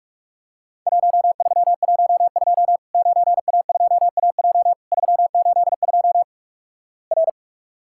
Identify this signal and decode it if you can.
Morse code, heard as a keyed tone, about 46 wpm, 700 Hz. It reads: 1312 9A2AJ 382 R
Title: Worked All Europe DX Contest - QTC Exchange - 45wpm